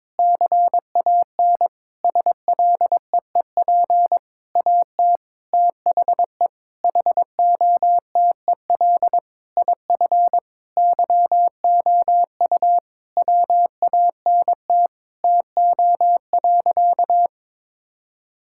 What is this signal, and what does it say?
Morse code, heard as a keyed tone, about 22 wpm, 700 Hz. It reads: CAN SLEEP AT THE HOTEL IF YOU WANT TO.
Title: Sentences from Top 1600 Words 22wpm